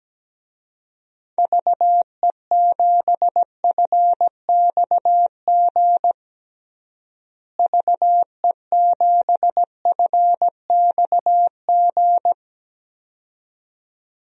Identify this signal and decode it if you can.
Morse code, heard as a keyed tone, about 17 wpm, 700 Hz. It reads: VE7FXG VE7FXG
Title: DXCC-100 Call Signs - 6 Characters - Repeated 2x - 17wpm